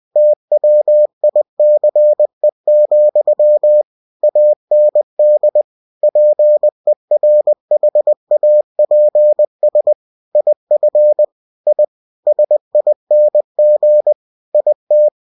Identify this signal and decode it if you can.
Morse code, heard as a keyed tone, about 20 wpm, 600 Hz. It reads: TWICE, AND PERHAPS IF I SING IT